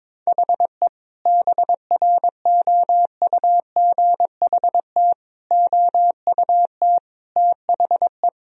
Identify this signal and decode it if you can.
Morse code, heard as a keyed tone, about 22 wpm, 700 Hz. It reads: HE BROUGHT OUT THE
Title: Sentences from Top 1000 Words - Review Part 3 - 22wpm